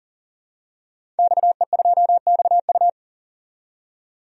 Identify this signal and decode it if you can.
Morse code, heard as a keyed tone, about 40 wpm, 700 Hz. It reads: XE2XU
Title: DXCC-100 Call Signs - 5 Characters - 40wpm